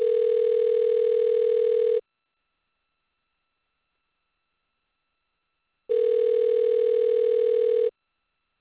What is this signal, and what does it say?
Ringback tone (North America), 440 + 480 Hz.